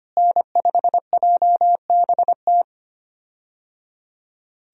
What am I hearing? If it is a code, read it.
Morse code, heard as a keyed tone, about 25 wpm, 700 Hz. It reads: N5JBT
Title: USA Call Signs - 5 Characters - Repeated 3x - 25wpm